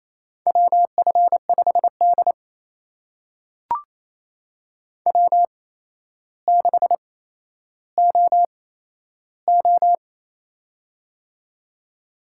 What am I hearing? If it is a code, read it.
Morse code, heard as a keyed tone, about 28 wpm, spaced as for 12 wpm, 700 Hz. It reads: WF5D W6OO
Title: USA Call Signs - 4 Characters - 8x Character Spacing - 28wpm